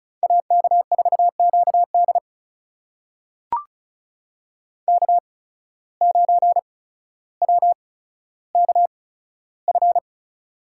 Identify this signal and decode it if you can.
Morse code, heard as a keyed tone, about 34 wpm, spaced as for 15 wpm, 700 Hz. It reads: AK4QD K9WKF
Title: USA Call Signs - 5 Characters - 8x Character Spacing - 35wpm